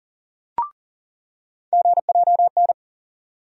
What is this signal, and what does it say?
Morse code, heard as a keyed tone, about 40 wpm, 700 Hz. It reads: GJN
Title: USA Call Sign Postfixes - 3 Letters 40wpm